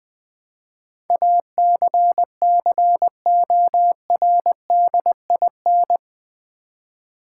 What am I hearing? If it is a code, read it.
Morse code, heard as a keyed tone, about 20 wpm, 700 Hz. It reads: ACCORDIN
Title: Build A Word - 4 Letters At A Time - 11-Letter Words - 20wpm